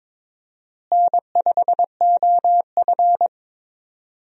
Morse code, heard as a keyed tone, about 22 wpm, 700 Hz. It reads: N5OF
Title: USA Call Signs - 4 Characters - Repeated 3x - 22wpm